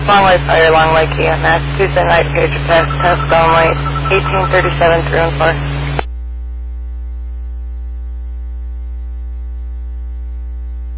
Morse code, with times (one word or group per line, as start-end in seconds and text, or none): none